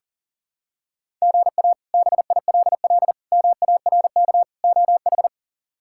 Morse code, heard as a keyed tone, about 40 wpm, 700 Hz. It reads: GA BILL MARK OH
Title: K1USN SST QSO Practice - 40wpm